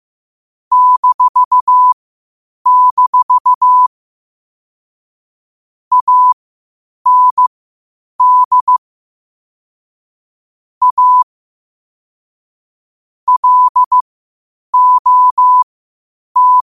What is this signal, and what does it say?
Morse code, heard as a keyed tone, about 15 wpm, spaced as for 8 wpm, 1000 Hz. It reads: -- AND A LOT
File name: qotdfiles / 8wpm